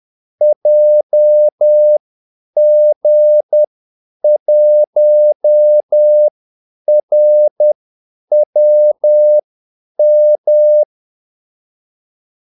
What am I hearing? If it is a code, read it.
Morse code, heard as a keyed tone, about 10 wpm, spaced as for 8 wpm, 600 Hz. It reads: JG1RWM
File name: Callsigns-10-8wpm